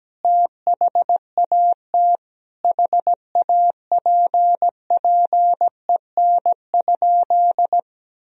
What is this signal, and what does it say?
Morse code, heard as a keyed tone, about 17 wpm, 700 Hz. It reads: THAT HAPPEN?
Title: Sentences from Top 400 Words and Abbreviations 17wpm